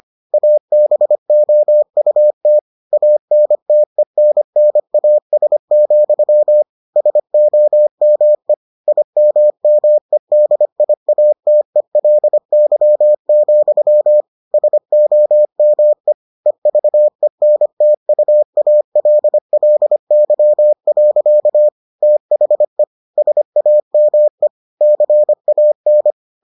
Morse code, heard as a keyed tone, about 25 wpm, 600 Hz. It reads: ABOUT ANTENNAS, SOME IMMEDIATELY, SOME EVENTUALLY. THE SAME CAN